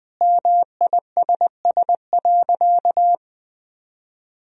Morse code, heard as a keyed tone, about 21 wpm, 700 Hz. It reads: MISS.
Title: Sentences from Top 1000 Words - Review Part 3 - 20wpm